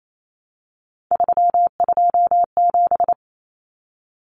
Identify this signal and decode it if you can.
Morse code, heard as a keyed tone, about 28 wpm, 700 Hz. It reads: 327
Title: Common Cut Numbers - 3-Digit 28wpm